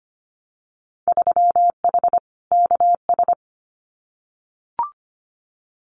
Morse code, heard as a keyed tone, about 25 wpm, 700 Hz. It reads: 3H KS
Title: Winter Field Day Exchanges 25wpm